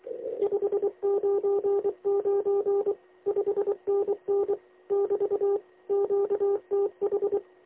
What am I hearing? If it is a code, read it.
Morse code, heard as a keyed tone, about 23 wpm, 400 Hz. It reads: 599 5NN <BT> QTH
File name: lx1nj- -1005utc-10110khz2-49235